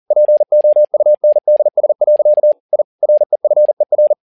Morse code, heard as a keyed tone, about 40 wpm, 600 Hz. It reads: POUNDS. I REFER
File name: learning center / tower trouble / tower touble 40wpm